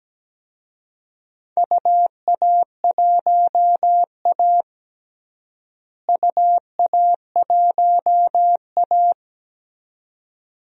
Morse code, heard as a keyed tone, about 17 wpm, 700 Hz. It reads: UA1A UA1A
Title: DXCC-100 Call Signs - 4 Characters - Repeated 2x - 17wpm